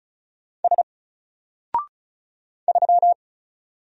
Morse code, heard as a keyed tone, about 36 wpm, 700 Hz. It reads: S 3
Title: Single Character (Letter:Number:Punctuation:Prosign) Alt 35wpm